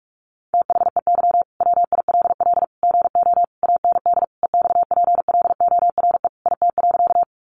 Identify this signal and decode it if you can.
Morse code, heard as a keyed tone, about 44 wpm, 700 Hz. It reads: THEY WILL GO AND EXPLORE IT.
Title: Sentences from Top 1100 Words 45wpm